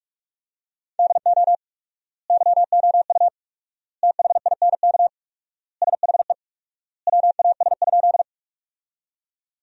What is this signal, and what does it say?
Morse code, heard as a keyed tone, about 47 wpm, 700 Hz. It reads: DO YOU THINK SHE WAS?
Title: 4x Word-Spacing - Sentences from Top 200 Words - 45wpm